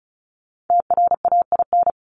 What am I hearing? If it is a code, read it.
Morse code, heard as a keyed tone, about 34 wpm, 700 Hz. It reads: TRAIN
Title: ING Suffix - Top 100 Words with Base Words 35wpm